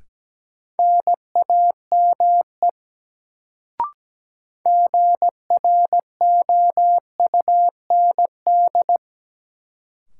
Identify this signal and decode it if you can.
Morse code, heard as a keyed tone, about 17 wpm, 700 Hz. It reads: NAME GROUND
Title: Sets of 3 Words - New Word Drill - 17wpm